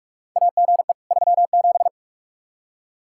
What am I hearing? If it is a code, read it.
Morse code, heard as a keyed tone, about 41 wpm, 700 Hz. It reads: AGE 37
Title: K1USN SST QSO Practice 45wpm